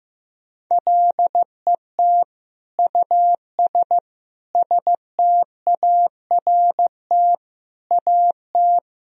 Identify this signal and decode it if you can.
Morse code, heard as a keyed tone, about 15 wpm, 700 Hz. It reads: LET US START AT
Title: Sentences from Top 1000 Words - Review Part 1 - 15wpm